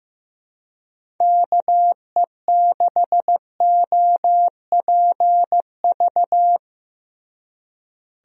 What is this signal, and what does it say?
Morse code, heard as a keyed tone, about 15 wpm, 700 Hz. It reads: KE6OPV
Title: USA Call Signs - 6 Characters - Repeated 2x - 15wpm